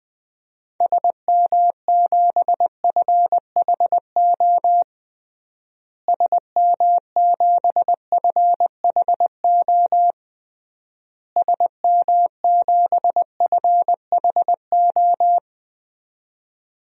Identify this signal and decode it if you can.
Morse code, heard as a keyed tone, about 20 wpm, 700 Hz. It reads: SM7FHO SM7FHO SM7FHO
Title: DXCC-100 Call Signs - 6 Characters - Repeated 3x - 20wpm